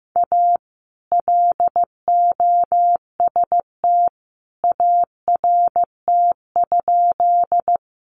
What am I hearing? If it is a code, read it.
Morse code, heard as a keyed tone, about 15 wpm, 700 Hz. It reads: A LOST ART?
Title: Sentences from Top 1400 Words 15wpm